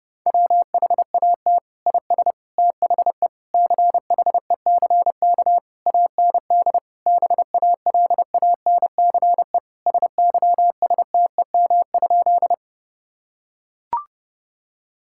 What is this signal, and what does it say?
Morse code, heard as a keyed tone, about 30 wpm, 700 Hz. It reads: WHAT IS THE CHECK AND BALANCE SYSTEM?
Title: Sentences from Top 1000 Words - Review Part 6 - 30wpm